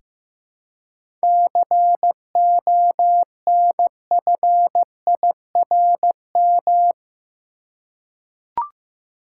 Morse code, heard as a keyed tone, about 15 wpm, 700 Hz. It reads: CONFIRM
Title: Seven Letter Words - 15wpm